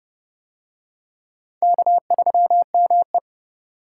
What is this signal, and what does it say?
Morse code, heard as a keyed tone, about 28 wpm, 700 Hz. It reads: K3ME